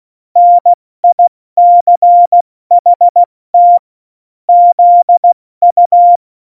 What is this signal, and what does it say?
Morse code, heard as a keyed tone, about 16 wpm, spaced as for 14 wpm, 700 Hz. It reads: NICHT ZU
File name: Morsekurs / Witze / Witz 23. April 2019, 70 F80